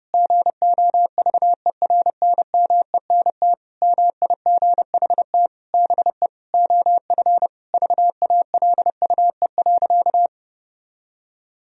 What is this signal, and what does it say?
Morse code, heard as a keyed tone, about 30 wpm, 700 Hz. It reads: GOVERNMENT MIGHT BE OF VALUE.
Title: Sentences from Top 1000 Words - Review Part 3 - 30wpm